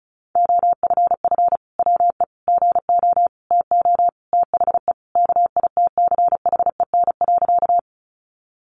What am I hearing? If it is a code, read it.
Morse code, heard as a keyed tone, about 34 wpm, 700 Hz. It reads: OFF WE GO TO THE KITCHEN.
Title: Sentences from Top 1400 Words 35wpm